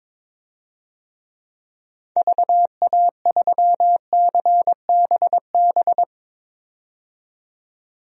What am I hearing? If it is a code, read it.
Morse code, heard as a keyed tone, about 22 wpm, 700 Hz. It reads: VA3CBB
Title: DXCC-100 Call Signs - 6 Characters - 22wpm